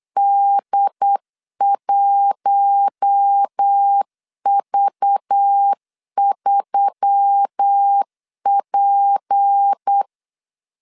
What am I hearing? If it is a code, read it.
Morse code, heard as a keyed tone, about 8 wpm, 800 Hz. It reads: D1V3P
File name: LEZIONI / 6 PKX0 / 6 PKX0 8wpm 3341